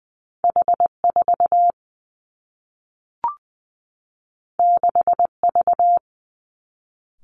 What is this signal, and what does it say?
Morse code, heard as a keyed tone, about 20 wpm, 700 Hz. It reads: H 4 6 V
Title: Two Challenging Letter-Number Combinations 20wpm